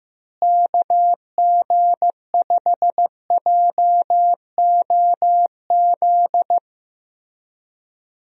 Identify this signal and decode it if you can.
Morse code, heard as a keyed tone, about 15 wpm, 700 Hz. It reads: KG5JOZ
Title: USA Call Signs - 6 Characters - Repeated 2x - 15wpm